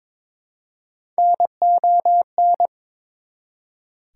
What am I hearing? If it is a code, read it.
Morse code, heard as a keyed tone, about 21 wpm, 700 Hz. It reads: NON
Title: Prefixes and Suffixes 22wpm